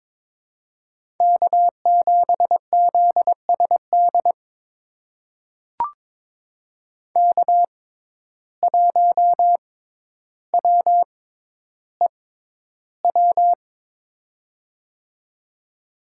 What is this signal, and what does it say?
Morse code, heard as a keyed tone, about 23 wpm, spaced as for 11 wpm, 700 Hz. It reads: K7ZSD K1WEW
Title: USA Call Signs - 5 Characters - 6x Character Spacing - 22wpm